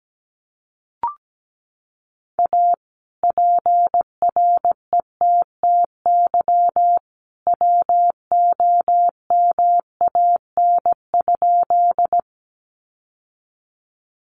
Morse code, heard as a keyed tone, about 17 wpm, 700 Hz. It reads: A PRETTY WOMAN?
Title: Sentences from Top 1100 Words 17wpm